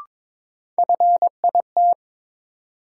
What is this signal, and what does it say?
Morse code, heard as a keyed tone, about 21 wpm, 700 Hz. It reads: FIT
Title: Sentences from Top 800 Words - New Word Drill - 22wpm